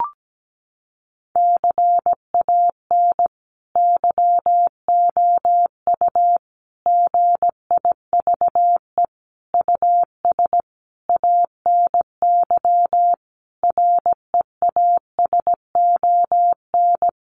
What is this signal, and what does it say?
Morse code, heard as a keyed tone, about 17 wpm, 700 Hz. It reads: CAN YOU GIVE US ANY REASON
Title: Sentences from Top 1100 Words 17wpm